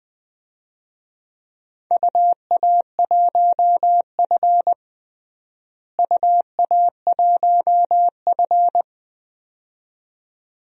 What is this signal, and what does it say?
Morse code, heard as a keyed tone, about 20 wpm, 700 Hz. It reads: UA1F UA1F
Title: DXCC-100 Call Signs - 4 Characters - Repeated 2x - 20wpm